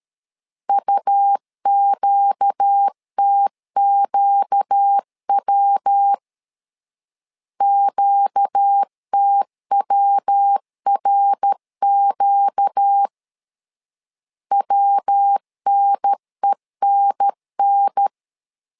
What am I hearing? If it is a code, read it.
Morse code, heard as a keyed tone, about 13 wpm, 800 Hz. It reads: UQTQW QTWRQ WNENN
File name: LEZIONI / 4 Q 12wpm 1833